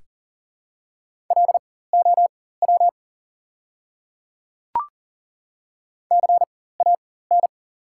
Morse code, heard as a keyed tone, about 38 wpm, spaced as for 18 wpm, 690 Hz. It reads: LOW CAN
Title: 4x Character Spacing - Top 300 Words - 40wpm